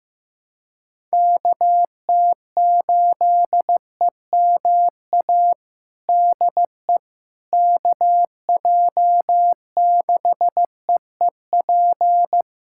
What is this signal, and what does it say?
Morse code, heard as a keyed tone, about 15 wpm, 700 Hz. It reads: KT8EMA DE KJ6EEP
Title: USA Callsign DE Callsign 15wpm